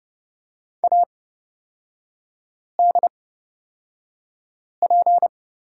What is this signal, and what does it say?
Morse code, heard as a keyed tone, about 30 wpm, spaced as for 5 wpm, 700 Hz. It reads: ADP
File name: Lesson 37e - Letter x - Three Characters 30wpm